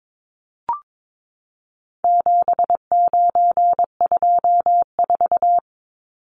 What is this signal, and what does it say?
Morse code, heard as a keyed tone, about 22 wpm, 700 Hz. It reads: 7924
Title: Numbers - 4-Digit 22wpm